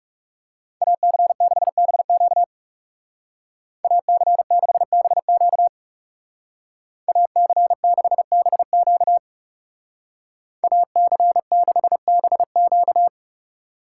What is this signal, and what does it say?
Morse code, heard as a keyed tone, about 39 wpm, 700 Hz. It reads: AC6BQ AC6BQ AC6BQ AC6BQ